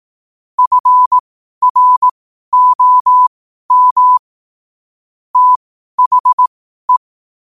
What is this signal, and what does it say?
Morse code, heard as a keyed tone, about 18 wpm, spaced as for 12 wpm, 1000 Hz. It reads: FROM THE
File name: qotdfiles / 12wpm